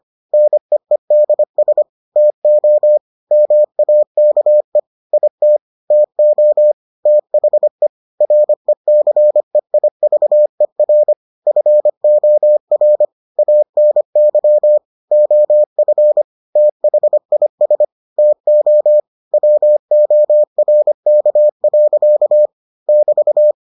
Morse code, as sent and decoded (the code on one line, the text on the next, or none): -. . -.. ... / - --- / -- .- -.- . / .. - / - --- / - .... / .-. -.-. .. ...- . .-. / ..-. --- .-. / .- -. -.-- / --- ..-. / - .... .. ... / - --- / .-- --- .-. -.- .-.-.- / -...-
NEEDS TO MAKE IT TO THE RECEIVER FOR ANY OF THIS TO WORK. <BT>